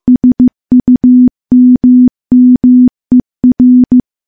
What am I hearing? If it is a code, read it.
Morse code, heard as a keyed tone, about 16 wpm, 250 Hz. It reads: SUMMER
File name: morse-training-15WPM